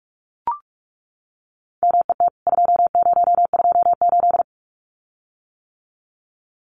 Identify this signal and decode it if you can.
Morse code, heard as a keyed tone, about 44 wpm, 700 Hz. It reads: MET 2028
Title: CWT Exchange 45wpm